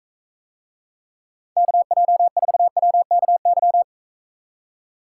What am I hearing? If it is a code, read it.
Morse code, heard as a keyed tone, about 45 wpm, 700 Hz. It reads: KJ4WKY